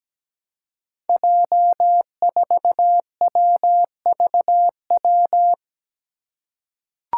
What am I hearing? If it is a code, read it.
Morse code, heard as a keyed tone, about 17 wpm, 700 Hz. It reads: J4WVW
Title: Five Challenging Letter-Number Combinations 17wpm